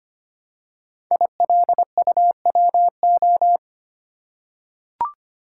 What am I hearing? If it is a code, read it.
Morse code, heard as a keyed tone, about 25 wpm, 700 Hz. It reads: ILUWO
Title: Code Groups - 5 Letters - 25wpm